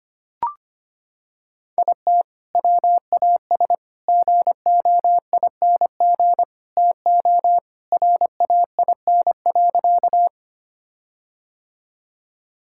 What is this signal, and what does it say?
Morse code, heard as a keyed tone, about 25 wpm, 700 Hz. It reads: IT WAS GOING TO RAIN.
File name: practice / 25wpm / Sentences with 4-Word Phrases - 25wpm